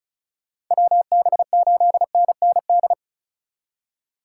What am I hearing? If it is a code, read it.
Morse code, heard as a keyed tone, about 43 wpm, 700 Hz. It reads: WB8NND
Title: Speed Racing - USA Call Signs 35wpm